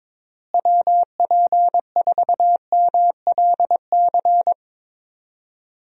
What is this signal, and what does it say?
Morse code, heard as a keyed tone, about 22 wpm, 700 Hz. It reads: WP4MLC
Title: USA Call Signs - 6 Characters - Repeated 2x - 22wpm